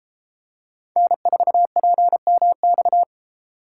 Morse code, heard as a keyed tone, about 32 wpm, 700 Hz. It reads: N4PMX